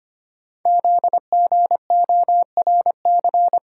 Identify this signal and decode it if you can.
Morse code, heard as a keyed tone, about 25 wpm, 700 Hz. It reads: ZGORC
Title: Code Groups - 5 Letters - 25wpm